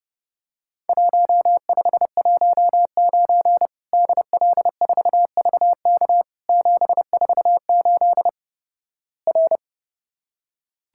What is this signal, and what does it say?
Morse code, heard as a keyed tone, about 30 wpm, 700 Hz. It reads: 1519 DL4VK 748 R
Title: Worked All Europe DX Contest - QTC Exchange - 30wpm